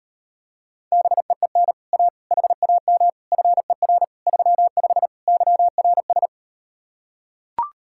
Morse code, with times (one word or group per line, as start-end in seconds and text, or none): 0.9-1.7 BEEN
1.9-2.1 A
2.3-3.1 HAM
3.3-4.0 FER
4.3-5.1 35
5.3-6.3 YRS